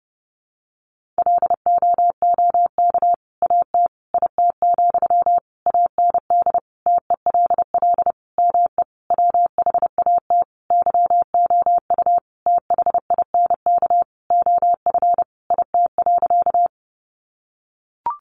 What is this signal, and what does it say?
Morse code, heard as a keyed tone, about 30 wpm, 700 Hz. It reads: LOOK AT IT, AND TELL ME WHAT YOU THINK OF IT.
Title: Sentences from Top 200 Words 30wpm